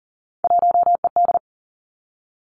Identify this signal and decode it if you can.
Morse code, heard as a keyed tone, about 38 wpm, 700 Hz. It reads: 1ED
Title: Three Challenging Letter-Number Combinations 40wpm